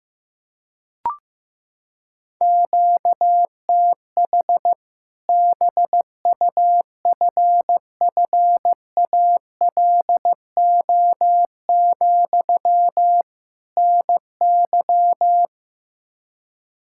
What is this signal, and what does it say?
Morse code, heard as a keyed tone, about 15 wpm, 700 Hz. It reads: QTH BUFFALO, NY
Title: Practice QSO - Exchange 1 - Beginning - 15wpm